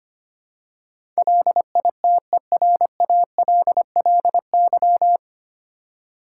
Morse code, heard as a keyed tone, about 25 wpm, 700 Hz. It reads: LITERALLY
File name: practice / 25wpm / 9 Letter Words - First 3 Letters Given - 25wpm